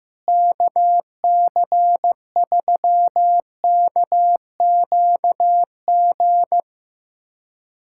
Morse code, heard as a keyed tone, about 15 wpm, 700 Hz. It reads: KC3KQG
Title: USA Callsign DE Callsign - 6 Characters - 15wpm